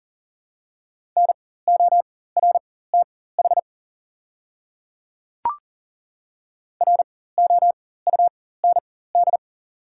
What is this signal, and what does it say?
Morse code, heard as a keyed tone, about 40 wpm, spaced as for 19 wpm, 690 Hz. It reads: NORTH ROUND
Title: 4x Character Spacing - Top 500 Words - 40wpm